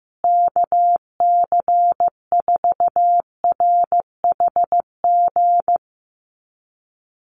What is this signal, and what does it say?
Morse code, heard as a keyed tone, about 15 wpm, 700 Hz. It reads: KC4RHG
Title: USA Call Signs Repeated 2x 15wpm